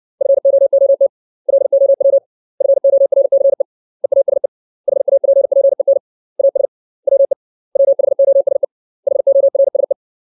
Morse code, heard as a keyed tone, about 60 wpm, 550 Hz. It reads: FOOT LOW FORCE EASE HAPPEN AS WE WHOSE HORSE
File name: 632 words / 632 Words 60WPM